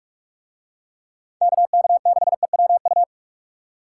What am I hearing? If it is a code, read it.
Morse code, heard as a keyed tone, about 45 wpm, 700 Hz. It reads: KK6EWU